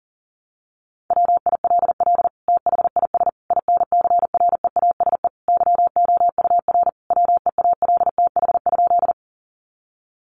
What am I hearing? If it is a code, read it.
Morse code, heard as a keyed tone, about 40 wpm, 700 Hz. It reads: WILL THIS INCREASE YOUR WEALTH?
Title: Sentences from Top 1200 Words 40wpm